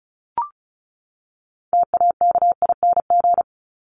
Morse code, heard as a keyed tone, about 34 wpm, 700 Hz. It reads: TAKING
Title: ING Suffix - Top 100 Words with Base Words 35wpm